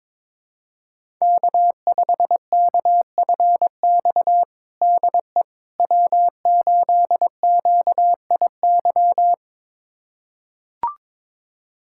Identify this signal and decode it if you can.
Morse code, heard as a keyed tone, about 22 wpm, 700 Hz. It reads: K5KFX DE W8QIY
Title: USA Callsign DE Callsign - 5 Characters - 22wpm